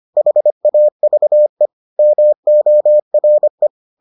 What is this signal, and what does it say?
Morse code, heard as a keyed tone, about 25 wpm, 600 Hz. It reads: HAVE MORE